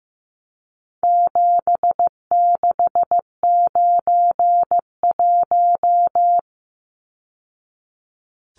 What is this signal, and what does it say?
Morse code, heard as a keyed tone, about 15 wpm, 700 Hz. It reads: 7691
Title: Numbers - 4-Digit 15wpm